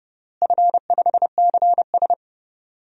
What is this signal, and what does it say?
Morse code, heard as a keyed tone, about 30 wpm, 700 Hz. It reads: F5CS
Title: DXCC-100 Call Signs - 4 Characters - Repeated 3x - 30wpm